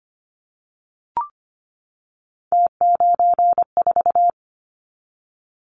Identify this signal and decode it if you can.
Morse code, heard as a keyed tone, about 25 wpm, 700 Hz. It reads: T94
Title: Callsign Prefixes for Top 50 DXCC - 25wpm